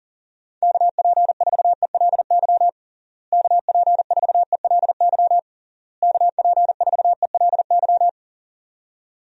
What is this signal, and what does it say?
Morse code, heard as a keyed tone, about 40 wpm, 700 Hz. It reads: KP4ELY KP4ELY KP4ELY
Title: USA Call Signs - 6 Characters - Repeated 3x - 40wpm